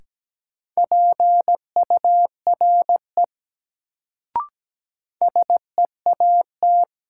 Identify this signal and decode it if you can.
Morse code, heard as a keyed tone, about 17 wpm, 700 Hz. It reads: PURE SEAT
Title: Sentences from Top 1500 Words - New Word Drill - 17wpm